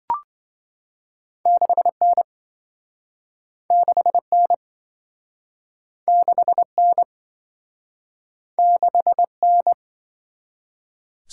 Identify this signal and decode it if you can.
Morse code, heard as a keyed tone, about 26 wpm, 700 Hz. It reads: 6N 6N 6N 6N